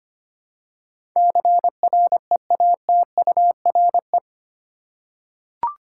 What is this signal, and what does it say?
Morse code, heard as a keyed tone, about 25 wpm, 700 Hz. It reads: CREATURE
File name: Eight Letter Words - 25wpm